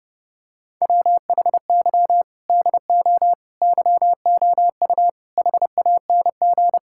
Morse code, heard as a keyed tone, about 30 wpm, 700 Hz. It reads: WHY DO YOU HANG
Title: Sentences from Top 1600 Words 30wpm